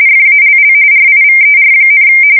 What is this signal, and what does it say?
RTTY, about 45.45 baud. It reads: FALL RIVER,
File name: fsk170 / USS-Massachusetts